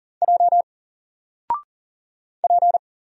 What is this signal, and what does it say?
Morse code, heard as a keyed tone, about 40 wpm, 700 Hz. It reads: J P
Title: Single Letter-Number Alt 40wpm